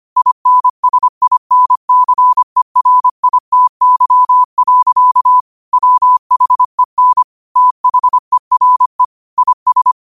Morse code, heard as a keyed tone, about 25 wpm, 1000 Hz. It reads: INSINCERITY. WHEN THERE IS
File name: 25wpm